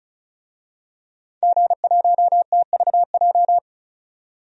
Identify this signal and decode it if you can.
Morse code, heard as a keyed tone, about 34 wpm, 700 Hz. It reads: G1TVJ
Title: DXCC-100 Call Signs - 5 Characters - 35wpm